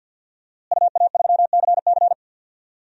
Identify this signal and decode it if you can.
Morse code, heard as a keyed tone, about 53 wpm, 700 Hz. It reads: UA3CC
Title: DXCC-100 Call Signs - 5 Characters - Repeated 2x - 50wpm